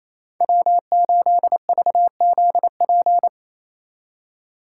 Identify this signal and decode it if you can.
Morse code, heard as a keyed tone, about 30 wpm, 700 Hz. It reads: W8VZP
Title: Speed Racing - USA 1x3 Call Signs 28wpm